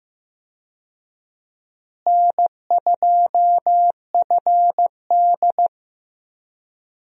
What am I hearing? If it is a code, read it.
Morse code, heard as a keyed tone, about 15 wpm, 700 Hz. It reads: N2FD
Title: USA Call Signs - 4 Characters - Repeated 3x - 15wpm